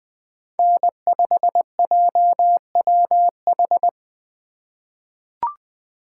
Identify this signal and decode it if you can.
Morse code, heard as a keyed tone, about 20 wpm, 700 Hz. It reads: N5JWH
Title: USA Call Signs - 5 Characters - 3x Character Spacing - 20wpm